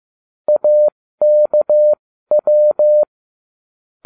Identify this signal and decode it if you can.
Morse code, heard as a keyed tone, about 15 wpm, spaced as for 13 wpm, 600 Hz. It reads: AKW